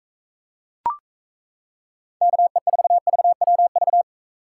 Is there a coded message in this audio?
Yes: Morse code from 2.2 to 4.0 s, heard as a keyed tone, about 37 wpm, 700 Hz. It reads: KE4VWV